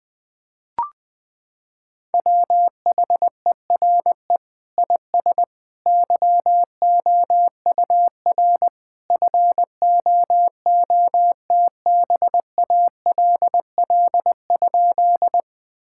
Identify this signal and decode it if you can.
Morse code, heard as a keyed tone, about 20 wpm, 700 Hz. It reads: WHERE IS YOUR FOOTBALL?
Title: Sentences from Top 1600 Words 20wpm